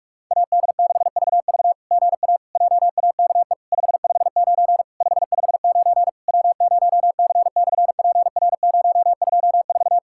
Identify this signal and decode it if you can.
Morse code, heard as a keyed tone, about 46 wpm, 700 Hz. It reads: AD6VV GA JAKE 559 559 W0C/PR024
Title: SOTA QSO Practice 45wpm